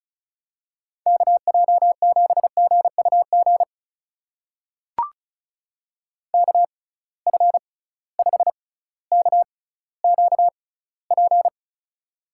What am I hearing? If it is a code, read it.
Morse code, heard as a keyed tone, about 35 wpm, spaced as for 18 wpm, 700 Hz. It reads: KJ7GUG KF5KQP
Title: USA Call Signs - 6 Characters - 6x Character Spacing - 35wpm